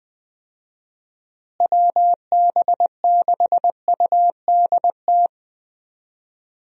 Morse code, heard as a keyed tone, about 20 wpm, 700 Hz. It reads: WB6UDT
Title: USA Call Signs - 6 Characters - Repeated 3x - 20wpm